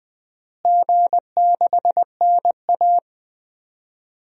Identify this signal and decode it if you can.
Morse code, heard as a keyed tone, about 20 wpm, 700 Hz. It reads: G6NA
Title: DXCC-100 Call Signs - 4 Characters - Repeated 2x - 20wpm